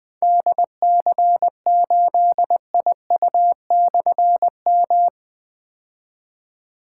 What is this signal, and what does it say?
Morse code, heard as a keyed tone, about 20 wpm, 700 Hz. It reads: DC8IU/M
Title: Difficult Call Signs 20wpm